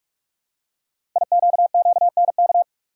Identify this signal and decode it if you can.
Morse code, heard as a keyed tone, about 44 wpm, 700 Hz. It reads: IQQNK
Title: Code Groups - 5 Letters - 45wpm